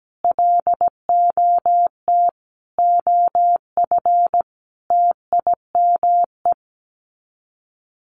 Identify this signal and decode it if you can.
Morse code, heard as a keyed tone, about 17 wpm, 700 Hz. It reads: LOT OF TIME
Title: Sets of 5 Words 17wpm